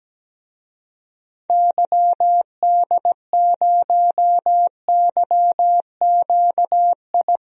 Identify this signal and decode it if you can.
Morse code, heard as a keyed tone, about 17 wpm, 700 Hz. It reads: YD0YQI
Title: DXCC-100 Call Signs - 6 Characters - Repeated 2x - 17wpm